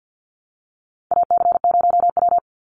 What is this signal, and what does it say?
Morse code, heard as a keyed tone, about 48 wpm, 700 Hz. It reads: AC0W
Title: Call Signs Repeated 2x - 50wpm